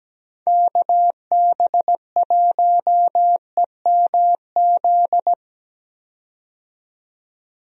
Morse code, heard as a keyed tone, about 17 wpm, 700 Hz. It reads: KB1EMZ
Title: USA Call Signs - 6 Characters - 17wpm